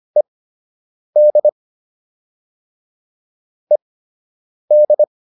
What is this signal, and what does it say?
Morse code, heard as a keyed tone, about 25 wpm, spaced as for 8 wpm, 600 Hz. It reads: ED ED